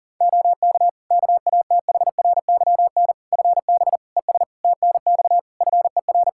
Morse code, heard as a keyed tone, about 37 wpm, 700 Hz. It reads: OK KATHRYN FB ES TNX FER
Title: K1USN SST QSO Practice 40wpm